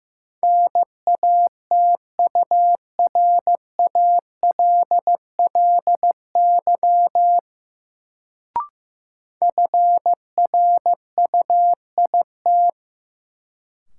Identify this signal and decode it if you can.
Morse code, heard as a keyed tone, about 15 wpm, 700 Hz. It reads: NATURALLY FRUIT
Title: Sentences from Top 1300 Words - New Word Drill - 15wpm